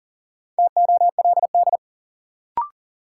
Morse code, heard as a keyed tone, about 40 wpm, 700 Hz. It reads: TOLD
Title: Sentences from Top 400 Words - New Word Drill - 40wpm